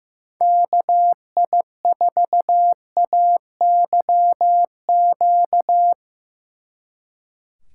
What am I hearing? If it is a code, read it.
Morse code, heard as a keyed tone, about 16 wpm, 700 Hz. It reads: KI4AYQ
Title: Speed Racing - USA Call Signs 15wpm